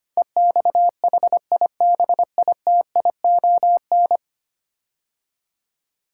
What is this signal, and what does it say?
Morse code, heard as a keyed tone, about 25 wpm, 700 Hz. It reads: EXHIBITION
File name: 10 Letter Words - First 3 Letters Given - 25wpm